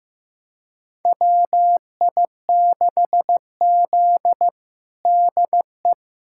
Morse code, heard as a keyed tone, about 15 wpm, 700 Hz. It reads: WI6Z DE
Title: USA Callsign DE Callsign - 4 Characters - 15wpm